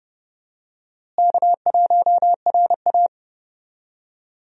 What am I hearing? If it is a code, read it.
Morse code, heard as a keyed tone, about 30 wpm, 700 Hz. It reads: K1RA
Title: USA Call Signs - 4 Characters - Repeated 2x - 30wpm